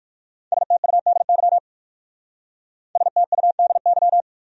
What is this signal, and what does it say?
Morse code, heard as a keyed tone, about 51 wpm, 700 Hz. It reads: STUDY STUDY
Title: Speed-Racing Top 300 Words 45wpm